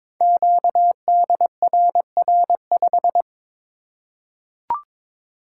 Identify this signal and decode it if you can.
Morse code, heard as a keyed tone, about 22 wpm, 700 Hz. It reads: QDRR5
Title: Code Groups - 5 Letter:Numbers - 22wpm